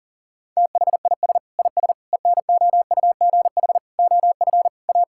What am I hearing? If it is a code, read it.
Morse code, heard as a keyed tone, about 37 wpm, 700 Hz. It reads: THIS IS ENOUGH OF A
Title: Sentences from Top 1000 Words - Review Part 5 - 40wpm